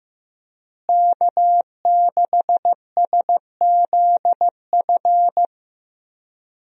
Morse code, heard as a keyed tone, about 16 wpm, 700 Hz. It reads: K6SZF
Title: USA Call Signs - 5 Characters - Repeated 3x - 15wpm